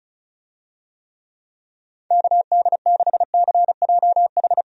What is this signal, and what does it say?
Morse code, heard as a keyed tone, about 36 wpm, 700 Hz. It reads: KD6CJH
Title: USA Call Signs - 6 Characters - 35wpm